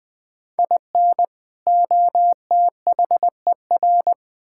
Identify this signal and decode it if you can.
Morse code, heard as a keyed tone, about 21 wpm, 700 Hz. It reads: IN OTHER
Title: Build A Phrase - 3-Word Phrases - 20wpm